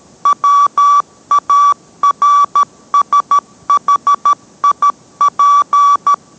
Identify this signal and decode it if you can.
Morse code, heard as a keyed tone, about 13 wpm, 1150 Hz. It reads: WARSHIP